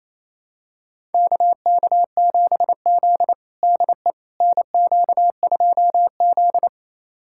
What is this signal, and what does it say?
Morse code, heard as a keyed tone, about 28 wpm, 700 Hz. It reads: KK7Z DE NQ2Z
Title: USA Callsign DE Callsign - 4 Characters - 28wpm